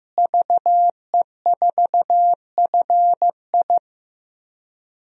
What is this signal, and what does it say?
Morse code, heard as a keyed tone, about 15 wpm, 700 Hz. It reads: VE4FI
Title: DXCC-100 Call Signs - 5 Characters - Repeated 3x - 15wpm